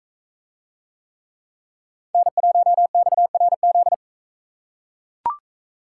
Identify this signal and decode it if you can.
Morse code, heard as a keyed tone, about 40 wpm, 700 Hz. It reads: N1XRZ